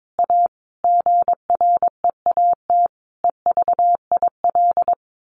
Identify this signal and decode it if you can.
Morse code, heard as a keyed tone, about 22 wpm, 700 Hz. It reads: A GREAT EVIL
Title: A Sound Of Thunder 22wpm